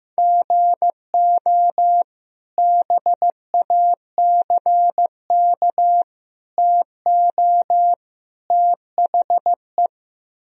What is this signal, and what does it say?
Morse code, heard as a keyed tone, about 16 wpm, 700 Hz. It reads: GO BACK TO THE
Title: Speed-Racing - Sets of 4 Words 15wpm